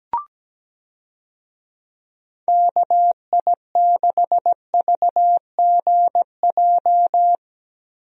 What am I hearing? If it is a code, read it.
Morse code, heard as a keyed tone, about 17 wpm, 700 Hz. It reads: KI6VGJ
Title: USA Call Signs - 6 Characters - Repeated 2x - 17wpm